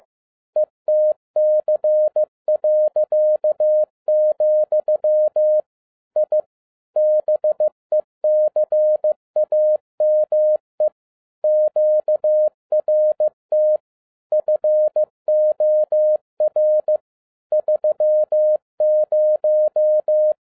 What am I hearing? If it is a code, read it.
Morse code, heard as a keyed tone, about 15 wpm, 610 Hz. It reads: ETC., I BECAME QRT FOR 30